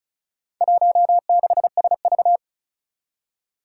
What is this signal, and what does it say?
Morse code, heard as a keyed tone, about 36 wpm, 700 Hz. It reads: 16SV
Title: Four Challenging Letter-Number Combinations 35wpm